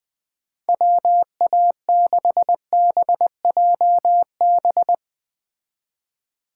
Morse code, heard as a keyed tone, about 20 wpm, 700 Hz. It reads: WA6BJB
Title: Speed Racing - USA Call Signs 17wpm